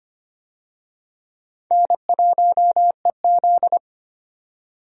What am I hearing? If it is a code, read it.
Morse code, heard as a keyed tone, about 25 wpm, 700 Hz. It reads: N1EZ
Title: USA Call Signs - 4 Characters - 25wpm